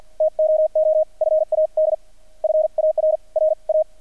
Morse code, heard as a keyed tone, about 48 wpm, 620 Hz. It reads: TOO FAD FAR R A
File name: archived / CRSnet